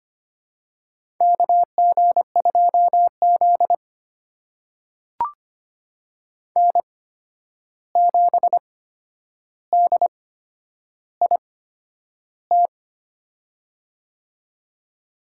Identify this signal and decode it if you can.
Morse code, heard as a keyed tone, about 25 wpm, spaced as for 10 wpm, 700 Hz. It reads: KG2Z N7DIT
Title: USA Call Signs - 8x Character Spacing - 25wpm